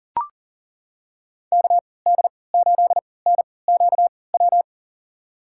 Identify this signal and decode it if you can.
Morse code, heard as a keyed tone, about 40 wpm, spaced as for 23 wpm, 700 Hz. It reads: KD8NQW
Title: USA Call Signs - 3x Character Spacing - 40wpm